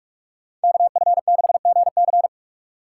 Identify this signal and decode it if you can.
Morse code, heard as a keyed tone, about 45 wpm, 700 Hz. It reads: KF6GC